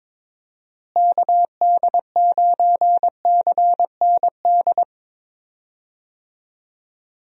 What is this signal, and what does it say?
Morse code, heard as a keyed tone, about 22 wpm, 700 Hz. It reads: KD9CND
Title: USA Call Signs - 6 Characters - Repeated 2x - 22wpm